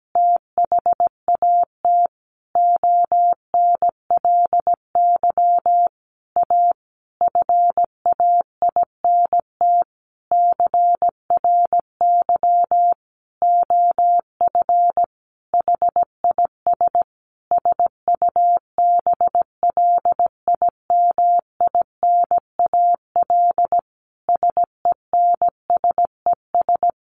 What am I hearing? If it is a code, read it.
Morse code, heard as a keyed tone, about 17 wpm, 700 Hz. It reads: THAT ONLY A FAINT CRY OF HIS SUBLIMINAL SENSES
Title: A Sound Of Thunder 17wpm